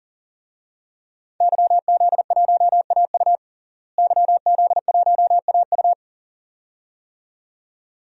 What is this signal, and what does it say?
Morse code, heard as a keyed tone, about 40 wpm, 700 Hz. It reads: YZ1AU YZ1AU
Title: DXCC-100 Call Signs - 5 Characters - Repeated 2x - 40wpm